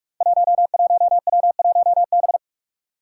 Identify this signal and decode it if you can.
Morse code, heard as a keyed tone, about 44 wpm, 700 Hz. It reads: 11W1B
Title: Five Challenging Letter-Number Combinations 45wpm